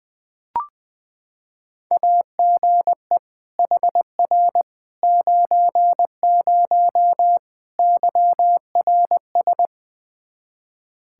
Morse code, heard as a keyed tone, about 20 wpm, 700 Hz. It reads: AGE HR 90 YRS
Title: Practice QSO - Exchange 3 - More Info - 20wpm